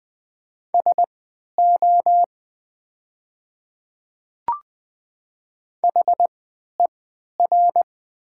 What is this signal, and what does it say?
Morse code, heard as a keyed tone, about 20 wpm, spaced as for 11 wpm, 700 Hz. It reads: SO HER
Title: 3x Character Spacing - Top 200 Words - 20wpm